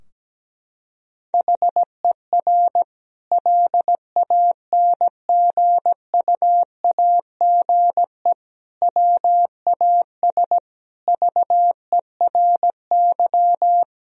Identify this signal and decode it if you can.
Morse code, heard as a keyed tone, about 17 wpm, 700 Hz. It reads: HER LANGUAGE WAS VERY
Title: Sentences from Top 1000 Words - Review Part 3 - 17wpm